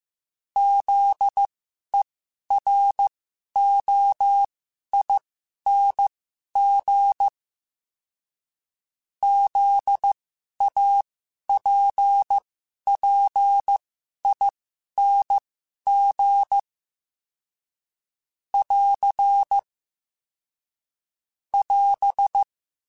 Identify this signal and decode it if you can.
Morse code, heard as a keyed tone, about 15 wpm, spaced as for 11 wpm, 800 Hz. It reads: ZEROING ZAPPING <AR> <AS>
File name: Special INGs-mp3s / Special INGs-mp3s / Sngl Ing @10